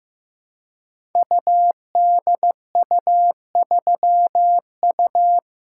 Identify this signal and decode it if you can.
Morse code, heard as a keyed tone, about 15 wpm, 700 Hz. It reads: UDU3U
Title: Code Groups - 5 Letter:Numbers - 15wpm